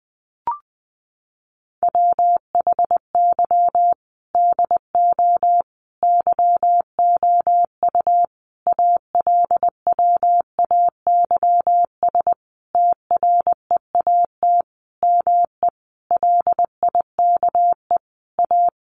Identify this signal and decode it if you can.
Morse code, heard as a keyed tone, about 20 wpm, 700 Hz. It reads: WHY DO YOU ALWAYS TREAT ME LIKE A
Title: Sentences from Top 1300 Words 20wpm